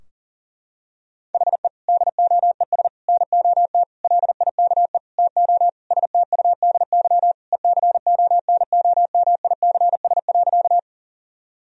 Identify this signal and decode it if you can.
Morse code, heard as a keyed tone, about 40 wpm, 700 Hz. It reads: HE DOES NOT LIKE TO STUDY ECONOMICS.
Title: Sentences from Top 1000 Words - Review Part 3 - 40wpm